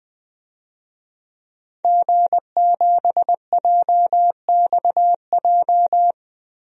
Morse code, heard as a keyed tone, about 20 wpm, 700 Hz. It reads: G7JXJ
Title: DXCC-100 Call Signs - 5 Characters - 20wpm